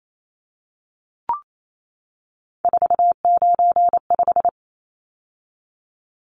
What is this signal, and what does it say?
Morse code, heard as a keyed tone, about 28 wpm, 700 Hz. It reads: 495
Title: Numbers - 3-Digit 28wpm